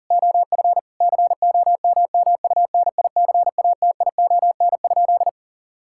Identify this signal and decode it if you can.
Morse code, heard as a keyed tone, about 37 wpm, 700 Hz. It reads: OF COMMUNICATION?
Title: Sentences from Top 1000 Words - Review Part 6 - 40wpm